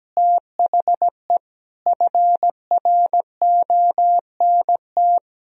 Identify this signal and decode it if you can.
Morse code, heard as a keyed tone, about 17 wpm, 700 Hz. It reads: THE FRONT
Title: Sentences from Top 1000 Words - Review Part 2 - 17wpm